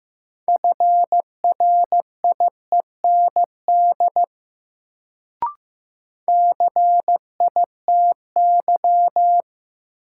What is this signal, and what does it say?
Morse code, heard as a keyed tone, about 15 wpm, 700 Hz. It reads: FRIEND CITY
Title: Sentences from Top 300 Words - New Word Drill - 15wpm